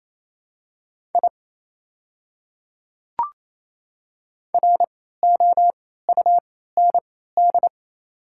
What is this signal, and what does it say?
Morse code, heard as a keyed tone, about 28 wpm, spaced as for 16 wpm, 690 Hz. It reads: I ROUND
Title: 3x Character Spacing - Top 200 Words - 28wpm